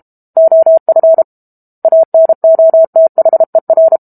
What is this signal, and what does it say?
Morse code, heard as a keyed tone, about 32 wpm, 650 Hz. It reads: OF ANOTHER
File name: Book Two / 06-Chapter 6 - 32 WPM